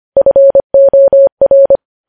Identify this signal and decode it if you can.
Morse code, heard as a keyed tone, about 25 wpm, 550 Hz. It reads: FOR
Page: news headlines in morse code at 25 wpm